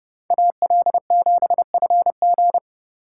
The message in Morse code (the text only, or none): AL7FG